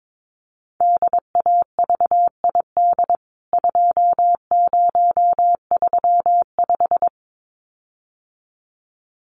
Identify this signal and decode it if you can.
Morse code, heard as a keyed tone, about 22 wpm, 700 Hz. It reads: DAVID 2035
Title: CWT Exchange 22wpm